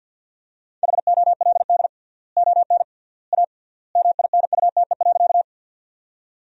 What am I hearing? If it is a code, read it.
Morse code, heard as a keyed tone, about 52 wpm, 700 Hz. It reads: HOLD ON A MINUTE.
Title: 3x Word-Spacing - Sentences from Top 500 Words - 50wpm